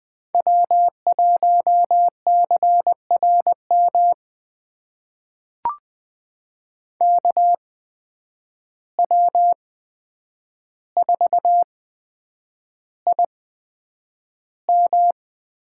Morse code, heard as a keyed tone, about 20 wpm, spaced as for 9 wpm, 700 Hz. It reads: W1CRM KW4IM
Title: USA Call Signs - 5 Characters - 8x Character Spacing - 20wpm